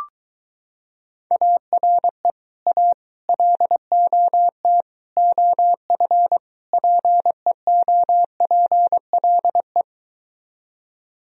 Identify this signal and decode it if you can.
Morse code, heard as a keyed tone, about 23 wpm, 700 Hz. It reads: ARE A LOT OF PEOPLE